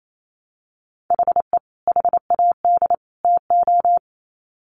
Morse code, heard as a keyed tone, about 28 wpm, 700 Hz. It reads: HE HAD TO
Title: Sets of 3 Words 28wpm